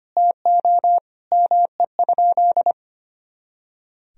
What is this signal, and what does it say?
Morse code, heard as a keyed tone, about 25 wpm, 700 Hz. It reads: TO ME?
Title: Sentences from Top 500 Words and Abbreviations 25wpm